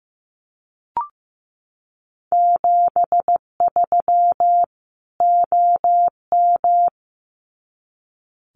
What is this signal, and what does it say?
Morse code, heard as a keyed tone, about 15 wpm, 700 Hz. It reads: 73 OM
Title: Common QSO Phrases 15wpm